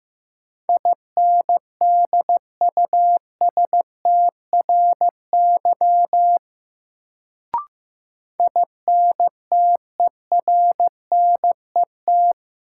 Morse code, heard as a keyed tone, about 15 wpm, 700 Hz. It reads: INDUSTRY INTERNET
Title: Sentences from Top 1000 Words - Review Part 4 - New Word Drill - 15wpm